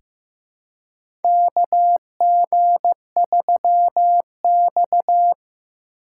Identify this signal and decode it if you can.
Morse code, heard as a keyed tone, about 15 wpm, 700 Hz. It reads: KG3X
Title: USA Call Signs - 4 Characters - 4x Character Spacing - 15wpm